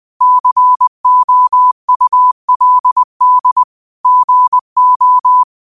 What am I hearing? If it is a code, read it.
Morse code, heard as a keyed tone, about 20 wpm, 1000 Hz. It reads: COULD GO
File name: qotdfiles / 20wpm